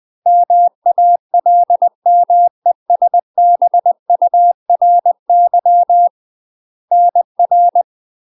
Morse code, heard as a keyed tone, about 20 wpm, 700 Hz. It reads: MALMESBURY NR
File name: UK MASTER QSOs TRAINING / 20WPM / UK MASTER QSOs 0022